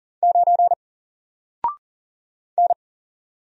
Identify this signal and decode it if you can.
Morse code, heard as a keyed tone, about 40 wpm, 700 Hz. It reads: 9 N
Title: Single Letter-Number Alt 40wpm